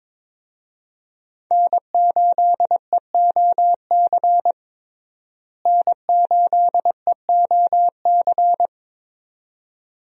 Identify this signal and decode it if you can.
Morse code, heard as a keyed tone, about 22 wpm, 700 Hz. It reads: N8EOC N8EOC
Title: USA Call Signs - 5 Characters - Repeated 2x - 22wpm